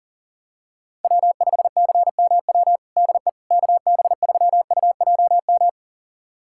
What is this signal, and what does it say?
Morse code, heard as a keyed tone, about 37 wpm, 700 Hz. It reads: W5CMW DE KB3UJM
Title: USA Callsign DE Callsign 40wpm